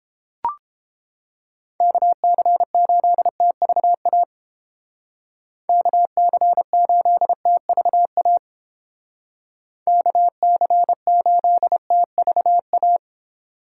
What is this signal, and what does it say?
Morse code, heard as a keyed tone, about 32 wpm, 700 Hz. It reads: KC8TVA KC8TVA KC8TVA